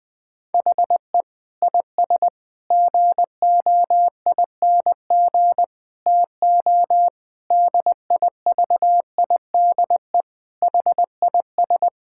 Morse code, heard as a keyed tone, about 20 wpm, 700 Hz. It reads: HE IS GOING TO DIVIDE HIS
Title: Sentences from Top 1600 Words 20wpm